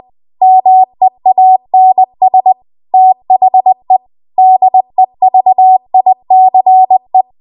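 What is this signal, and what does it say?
Morse code, heard as a keyed tone, about 20 wpm, 750 Hz. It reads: MEANS THE DEVICE
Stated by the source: Morse code practice at 20 WPM.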